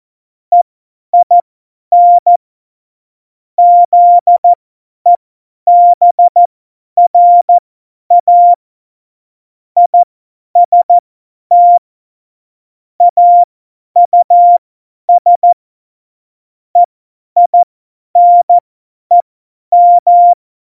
Morse code, heard as a keyed tone, about 14 wpm, spaced as for 10 wpm, 700 Hz. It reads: EIN ZEBRA IST AUS EINEM
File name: Morsekurs / Witze / Witz 15. Dezember 2020, 50 F70